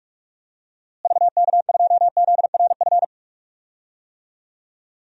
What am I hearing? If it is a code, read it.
Morse code, heard as a keyed tone, about 42 wpm, 700 Hz. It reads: VK2ZRF
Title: DXCC-100 Call Signs - 6 Characters - Repeated 2x - 45wpm